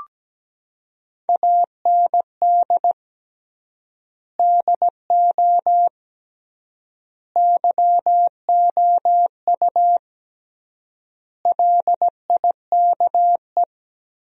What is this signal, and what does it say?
Morse code, heard as a keyed tone, about 17 wpm, 700 Hz. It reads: AND DO YOU LIKE
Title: 3x Word-Spacing - Sentences from Top 100 Words - 17wpm